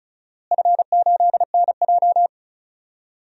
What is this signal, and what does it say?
Morse code, heard as a keyed tone, about 36 wpm, 700 Hz. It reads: F8NJ
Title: DXCC-100 Call Signs - 4 Characters - Repeated 3x - 35wpm